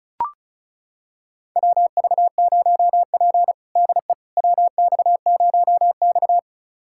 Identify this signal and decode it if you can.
Morse code, heard as a keyed tone, about 36 wpm, 700 Hz. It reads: WV0P DE WX0X
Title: USA Callsign DE Callsign - 4 Characters - 35wpm